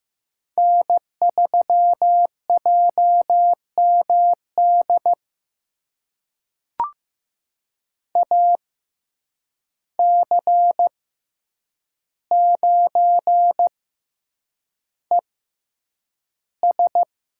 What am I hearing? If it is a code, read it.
Morse code, heard as a keyed tone, about 15 wpm, spaced as for 8 wpm, 700 Hz. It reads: N3JMD AC9ES
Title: USA Call Signs - 5 Characters - 6x Character Spacing - 15wpm